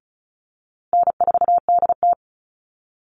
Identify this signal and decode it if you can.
Morse code, heard as a keyed tone, about 34 wpm, 700 Hz. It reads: N4DT
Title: Call Signs Repeated 2x - 35wpm